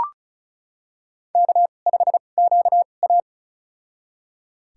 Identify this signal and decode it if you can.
Morse code, heard as a keyed tone, about 36 wpm, spaced as for 25 wpm, 690 Hz. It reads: K5QA
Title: USA Call Signs - 4 Characters - 2x Character Spacing - 35wpm